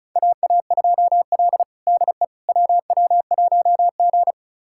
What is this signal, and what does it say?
Morse code, heard as a keyed tone, about 36 wpm, 700 Hz. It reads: AA2L DE WW1G
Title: USA Callsign DE Callsign - 4 Characters - 35wpm